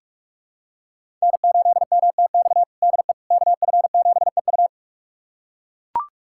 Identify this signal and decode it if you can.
Morse code, heard as a keyed tone, about 41 wpm, 700 Hz. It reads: N8MTX DE KF7EU
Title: USA Callsign DE Callsign - 5 Characters - 45wpm